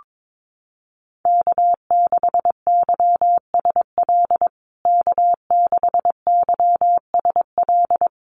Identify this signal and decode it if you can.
Morse code, heard as a keyed tone, about 22 wpm, 700 Hz. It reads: K6YSL K6YSL
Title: USA Call Signs Repeated 2x 22wpm